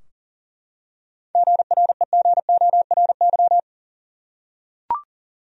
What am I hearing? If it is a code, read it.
Morse code, heard as a keyed tone, about 40 wpm, 700 Hz. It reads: GREGORY
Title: USA Common Names - 40wpm